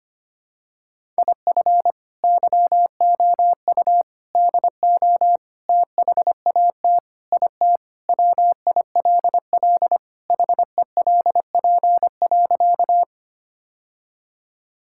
Morse code, heard as a keyed tone, about 25 wpm, 700 Hz. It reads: IF YOU DO THAT IT WILL HELP.
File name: practice / 25wpm / Sentences with 3-Word Phrases - 25wpm